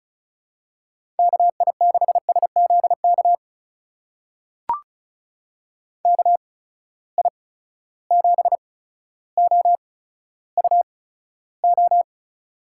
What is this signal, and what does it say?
Morse code, heard as a keyed tone, about 36 wpm, spaced as for 15 wpm, 700 Hz. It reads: KI6SZK KI7OUO
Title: USA Call Signs - 6 Characters - 8x Character Spacing - 35wpm